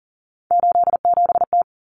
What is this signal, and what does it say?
Morse code, heard as a keyed tone, about 40 wpm, 700 Hz. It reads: 87T
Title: Common Cut Numbers - 3-Digit 40wpm